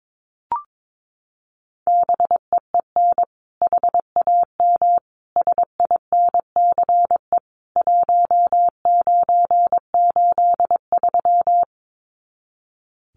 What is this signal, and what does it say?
Morse code, heard as a keyed tone, about 22 wpm, 700 Hz. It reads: BEEN HAM SINCE 1983
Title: Common QSO Phrases 22wpm